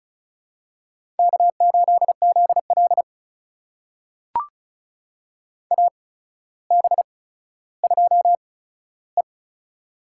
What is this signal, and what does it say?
Morse code, heard as a keyed tone, about 36 wpm, spaced as for 15 wpm, 700 Hz. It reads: K8ZL AB2E
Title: USA Call Signs - 4 Characters - 8x Character Spacing - 35wpm